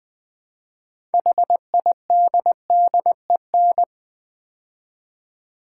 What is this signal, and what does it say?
Morse code, heard as a keyed tone, about 20 wpm, 700 Hz. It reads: HIDDEN
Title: Build A Word - 2 Letters At A Time - 6-Letter Words - 20wpm